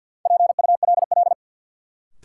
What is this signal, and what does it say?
Morse code, heard as a keyed tone, about 53 wpm, 700 Hz. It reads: PULL
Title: Sentences from Top 400 Words - New Word Drill - 50wpm